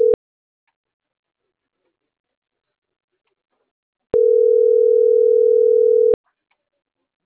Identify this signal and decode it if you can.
Ringback tone (North America), 440 + 480 Hz.